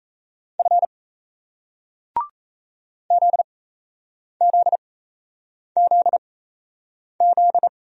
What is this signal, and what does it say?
Morse code, heard as a keyed tone, about 36 wpm, spaced as for 8 wpm, 700 Hz. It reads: F ZZZZ